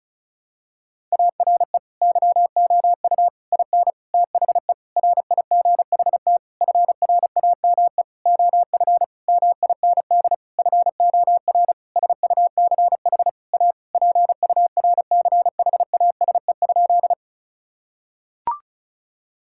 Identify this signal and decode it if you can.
Morse code, heard as a keyed tone, about 36 wpm, 700 Hz. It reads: ARE YOU IN THE RIGHT FRAME OF MIND FOR SUCH A PURCHASE?
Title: Sentences from Top 1000 Words - Review Part 6 - 35wpm